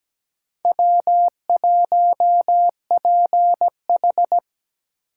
Morse code, heard as a keyed tone, about 20 wpm, 700 Hz. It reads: W1PH
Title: Speed Racing - USA 1x2 Call Signs 17wpm